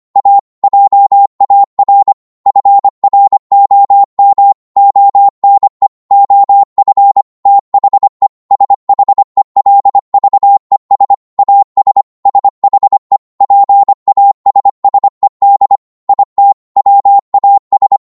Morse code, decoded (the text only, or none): A JAR FROM ONE OF THE SHELVES AS SHE PASSED IT WAS